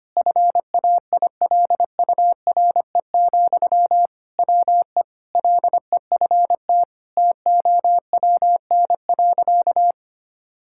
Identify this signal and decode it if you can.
Morse code, heard as a keyed tone, about 25 wpm, 700 Hz. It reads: FAILURE, WE LEFT TOWN.
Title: Sentences from Top 1000 Words - Review Part 6 - 25wpm